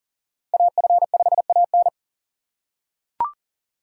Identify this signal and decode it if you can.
Morse code, heard as a keyed tone, about 40 wpm, 700 Hz. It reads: AF5AN
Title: USA Call Signs - 5 Characters - 2x Character Spacing - 40wpm